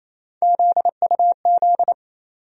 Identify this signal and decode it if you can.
Morse code, heard as a keyed tone, about 28 wpm, 700 Hz. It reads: ZUZ
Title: USA Call Sign Postfixes - 3 Letters 28wpm